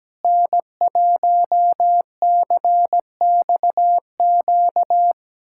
Morse code, heard as a keyed tone, about 17 wpm, 700 Hz. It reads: N1CXQ
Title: USA Callsign DE Callsign - 5 Characters - 17wpm